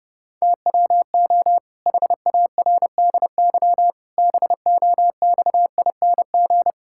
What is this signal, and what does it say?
Morse code, heard as a keyed tone, about 30 wpm, 700 Hz. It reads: TWO HARDY BOXING
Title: Pangram Challenge v2 - 30wpm